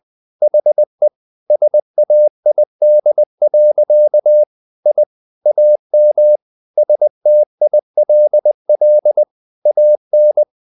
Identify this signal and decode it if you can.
Morse code, heard as a keyed tone, about 20 wpm, 600 Hz. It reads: HE SAID. I AM STILL AN
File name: the old man and the sea part 2 20wpm